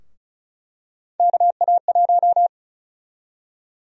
Morse code, heard as a keyed tone, about 36 wpm, 700 Hz. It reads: KA1
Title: USA Call Sign Prefixes - 35wpm